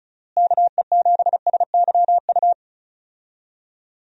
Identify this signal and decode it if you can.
Morse code, heard as a keyed tone, about 34 wpm, 700 Hz. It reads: KE7SYU
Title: USA Call Signs - 6 Characters - Repeated 2x - 35wpm